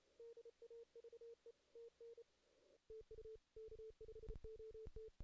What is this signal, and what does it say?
Morse code, heard as a keyed tone, about 32 wpm, 440 Hz. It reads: DAVE TN TU K5OT
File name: 2016 JAN CW NAQP / 2016 Jan naqp k5ot - 0-8